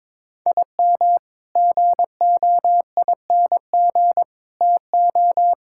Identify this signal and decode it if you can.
Morse code, heard as a keyed tone, about 22 wpm, 700 Hz. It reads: IM GOING TO
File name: practice / 22wpm / Sentences with 8-Word Phrases - 22wpm